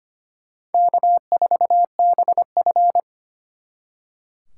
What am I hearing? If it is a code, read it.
Morse code, heard as a keyed tone, about 32 wpm, 700 Hz. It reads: K4BF